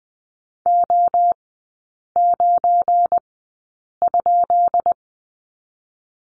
Morse code, heard as a keyed tone, about 20 wpm, spaced as for 8 wpm, 690 Hz. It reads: O9?
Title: 212 Lesson 34e - Number 8 - Three Characters 20wpm